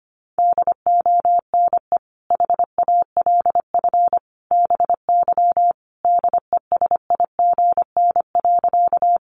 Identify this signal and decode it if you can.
Morse code, heard as a keyed tone, about 25 wpm, 700 Hz. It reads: DONE HALF BY DESIGN.
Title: Sentences from Top 800 Words 25wpm